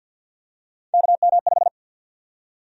Morse code, heard as a keyed tone, about 51 wpm, 700 Hz. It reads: KM5
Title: USA Call Sign Prefixes - Repeated 2x - 50wpm